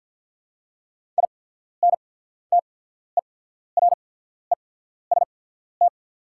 Morse code, heard as a keyed tone, about 48 wpm, spaced as for 14 wpm, 690 Hz. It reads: INTEREST
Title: 8x Character Spacing - Top 500 Words - 50wpm